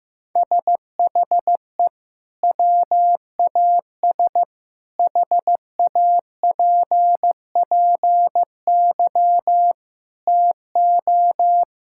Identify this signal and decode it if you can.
Morse code, heard as a keyed tone, about 15 wpm, 700 Hz. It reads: SHE WAS HAPPY TO
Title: Sentences from Top 1000 Words - Review Part 5 - 15wpm